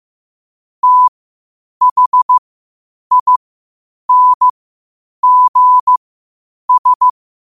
Morse code, heard as a keyed tone, about 15 wpm, spaced as for 8 wpm, 1000 Hz. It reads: THINGS
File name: qotdfiles / 8wpm